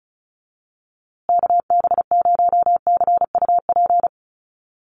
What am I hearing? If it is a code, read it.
Morse code, heard as a keyed tone, about 34 wpm, 700 Hz. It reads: KB0CUP
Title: USA Call Signs Repeated 3x 35wpm